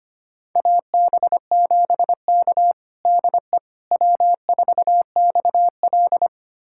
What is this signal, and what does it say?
Morse code, heard as a keyed tone, about 25 wpm, 700 Hz. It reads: AB7K DE W4XL
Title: USA Callsign DE Callsign - 4 Characters - 25wpm